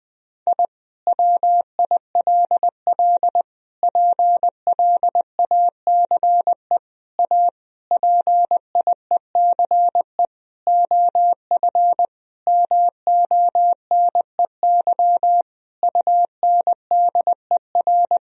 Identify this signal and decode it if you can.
Morse code, heard as a keyed tone, about 20 wpm, 700 Hz. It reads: I WILL PLACE A PIECE OF MONEY UNDER
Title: Sentences from Top 1000 Words - Review Part 2 - 20wpm